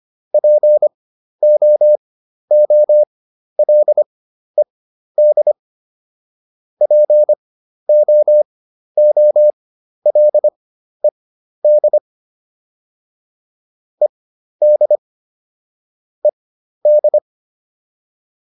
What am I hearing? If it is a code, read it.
Morse code, heard as a keyed tone, about 25 wpm, spaced as for 12 wpm, 600 Hz. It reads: POOLED POOLED ED ED
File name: ED6 12